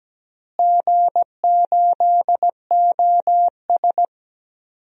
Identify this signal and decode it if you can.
Morse code, heard as a keyed tone, about 17 wpm, 700 Hz. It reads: G8OS
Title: DXCC-100 Call Signs - 4 Characters - Repeated 2x - 17wpm